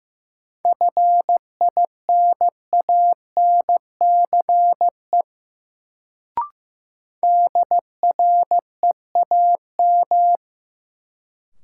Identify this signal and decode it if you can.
Morse code, heard as a keyed tone, about 15 wpm, 700 Hz. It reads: FINANCE DREAM
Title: Sentences from Top 1500 Words - New Word Drill Alt - 15wpm